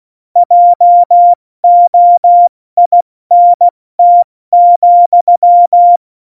Morse code, heard as a keyed tone, about 16 wpm, 700 Hz. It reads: JOINT,
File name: Morsekurs / Witze / Witz 08. Dezember 2020, 70 F80